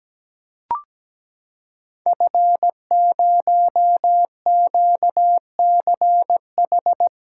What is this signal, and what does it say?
Morse code, heard as a keyed tone, about 17 wpm, 700 Hz. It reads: F0QCH
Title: Code Groups - 5 Letter:Numbers - 17wpm